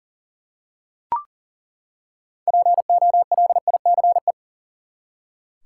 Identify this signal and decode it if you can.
Morse code, heard as a keyed tone, about 40 wpm, 700 Hz. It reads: POLICE
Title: Six Letter Words - 40wpm